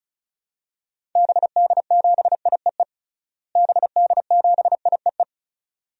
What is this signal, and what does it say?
Morse code, heard as a keyed tone, about 34 wpm, 700 Hz. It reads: BD7IEE BD7IEE
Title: DXCC-100 Call Signs - 6 Characters - Repeated 2x - 35wpm